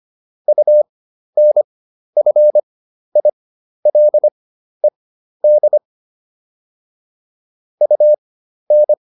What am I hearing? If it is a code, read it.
Morse code, heard as a keyed tone, about 25 wpm, spaced as for 12 wpm, 600 Hz. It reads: UNFILED UN